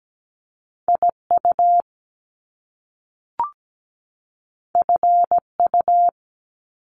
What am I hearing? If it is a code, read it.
Morse code, heard as a keyed tone, about 17 wpm, 700 Hz. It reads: IU FU
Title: Top Two Letter Combinations in Words 17wpm